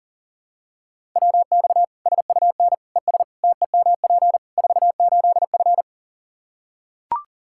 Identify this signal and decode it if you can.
Morse code, heard as a keyed tone, about 40 wpm, 700 Hz. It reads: WX SUN ES TEMP 48F
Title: K1USN SST QSO Practice 40wpm